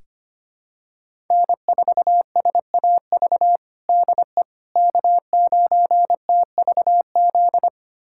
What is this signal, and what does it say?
Morse code, heard as a keyed tone, about 25 wpm, 700 Hz. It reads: N4SAV DE K9TVZ
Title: USA Callsign DE Callsign - 5 Characters - 25wpm